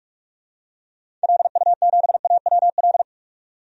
Morse code, heard as a keyed tone, about 46 wpm, 700 Hz. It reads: LU7AWL
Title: DXCC-100 Call Signs - 6 Characters - 45wpm